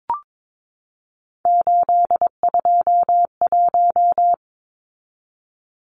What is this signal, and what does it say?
Morse code, heard as a keyed tone, about 22 wpm, 700 Hz. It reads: 821
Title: Numbers - 3-Digit 22wpm